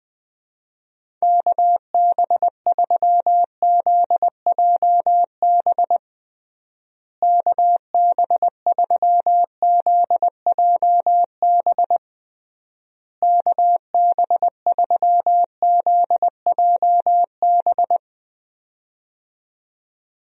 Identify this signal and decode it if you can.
Morse code, heard as a keyed tone, about 19 wpm, 700 Hz. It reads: KB3ZJB KB3ZJB KB3ZJB
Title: USA Call Signs - 6 Characters - Repeated 3x - 20wpm